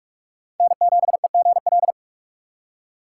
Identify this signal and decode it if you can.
Morse code, heard as a keyed tone, about 45 wpm, 700 Hz. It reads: N7EGL